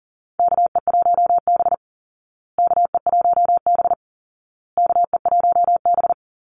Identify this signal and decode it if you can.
Morse code, heard as a keyed tone, about 40 wpm, 700 Hz. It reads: KE1B KE1B KE1B
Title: Call Signs Repeated 2x - 40wpm